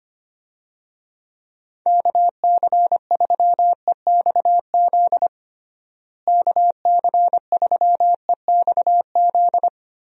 Morse code, heard as a keyed tone, about 25 wpm, 700 Hz. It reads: KC3EXZ KC3EXZ
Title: USA Call Signs - 6 Characters - Repeated 3x - 25wpm